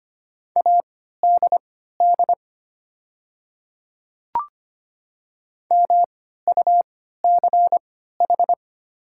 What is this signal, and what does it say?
Morse code, heard as a keyed tone, about 25 wpm, spaced as for 14 wpm, 700 Hz. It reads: ADD MUCH
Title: 3x Character Spacing - Top 300 Words - 25wpm